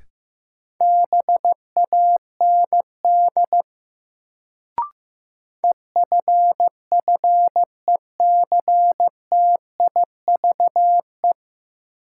Morse code, heard as a keyed tone, about 15 wpm, 700 Hz. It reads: BAND EFFECTIVE
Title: Sentences from Top 1600 Words - New Word Drill Alt - 15wpm